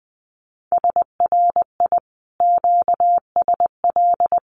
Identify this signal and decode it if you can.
Morse code, heard as a keyed tone, about 20 wpm, 700 Hz. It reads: SRI QSL
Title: Common QSO Phrases Encore 20wpm